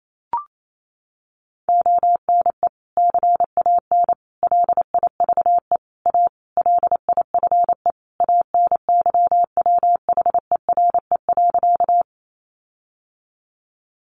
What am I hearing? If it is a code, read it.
Morse code, heard as a keyed tone, about 28 wpm, 700 Hz. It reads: ONE CAN LIVE A LIFE ANYWHERE.
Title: Sentences from Top 1400 Words 28wpm